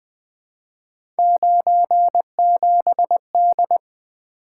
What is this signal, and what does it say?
Morse code, heard as a keyed tone, about 20 wpm, 700 Hz. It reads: 97D
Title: Three Letter-Number Combinations - 20wpm